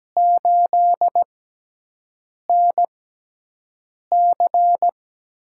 Morse code, heard as a keyed tone, about 17 wpm, spaced as for 6 wpm, 700 Hz. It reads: 8NC
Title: USA Call Signs - 5 Characters - 6x Character Spacing - 17wpm